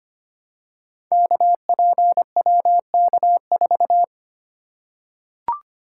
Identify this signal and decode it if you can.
Morse code, heard as a keyed tone, about 25 wpm, 700 Hz. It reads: KPWK4
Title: Code Groups - 5 Letter:Numbers - 25wpm